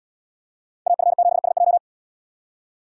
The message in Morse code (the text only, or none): AF7SJ